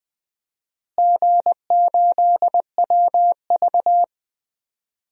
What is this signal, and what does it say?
Morse code, heard as a keyed tone, about 21 wpm, 700 Hz. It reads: G8WV
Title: DXCC-100 Call Signs - 4 Characters - Repeated 3x - 20wpm